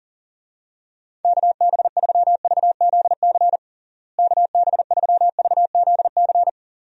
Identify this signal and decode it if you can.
Morse code, heard as a keyed tone, about 40 wpm, 700 Hz. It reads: KB3VZC KB3VZC
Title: USA Call Signs - 6 Characters - Repeated 3x - 40wpm